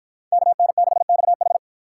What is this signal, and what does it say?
Morse code, heard as a keyed tone, about 53 wpm, 700 Hz. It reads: KN6XH